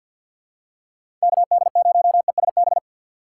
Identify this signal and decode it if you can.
Morse code, heard as a keyed tone, about 52 wpm, 700 Hz. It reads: KD0ESB
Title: USA Call Signs - 6 Characters - 6x Character Spacing - 50wpm